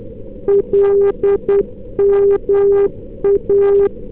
Morse code, heard as a keyed tone, about 10 wpm, 400 Hz. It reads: LMA
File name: NDB 311k